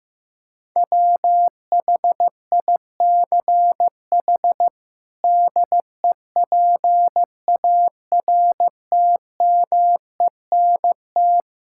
Morse code, heard as a keyed tone, about 16 wpm, 700 Hz. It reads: WHICH DEPARTMENT
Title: Sentences from Top 1000 Words - Review Part 5 - 15wpm